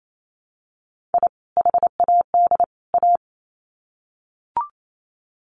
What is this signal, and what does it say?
Morse code, heard as a keyed tone, about 28 wpm, 700 Hz. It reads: I HAD A
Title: Sets of 3 Words 28wpm